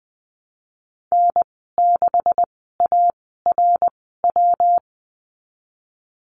Morse code, heard as a keyed tone, about 20 wpm, spaced as for 14 wpm, 700 Hz. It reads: N6ARW
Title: 128 Lesson 21g - Number 6 - Call Signs 20wpm